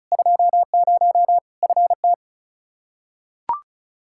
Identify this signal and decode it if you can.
Morse code, heard as a keyed tone, about 36 wpm, 700 Hz. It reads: 20 FT
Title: Practice QSO - Exchange 2 - Info - 35wpm